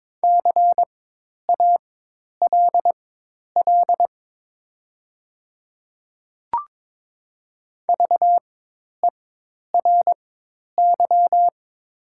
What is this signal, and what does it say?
Morse code, heard as a keyed tone, about 22 wpm, spaced as for 10 wpm, 700 Hz. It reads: CALL VERY
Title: 4x Character Spacing - Top 200 Words - 22wpm